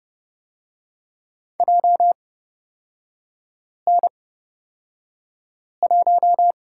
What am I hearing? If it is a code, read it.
Morse code, heard as a keyed tone, about 30 wpm, spaced as for 5 wpm, 700 Hz. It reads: JN1
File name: Lesson 36e - BT - Three Characters 30wpm